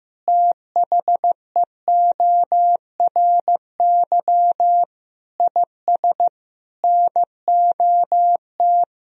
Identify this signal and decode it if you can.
Morse code, heard as a keyed tone, about 16 wpm, 700 Hz. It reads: THEORY IS NOT
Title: Sentences from Top 1000 Words - Review Part 3 - 15wpm